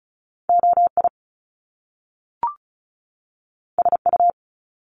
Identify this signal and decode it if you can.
Morse code, heard as a keyed tone, about 36 wpm, 700 Hz. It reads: OI SU
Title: Top Two Letter Combinations in Words 35wpm